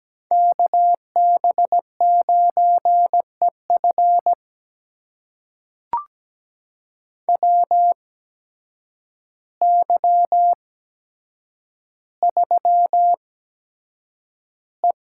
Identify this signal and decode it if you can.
Morse code, heard as a keyed tone, about 17 wpm, spaced as for 8 wpm, 700 Hz. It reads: KB9EF WY3E
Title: USA Call Signs - 8x Character Spacing - 17wpm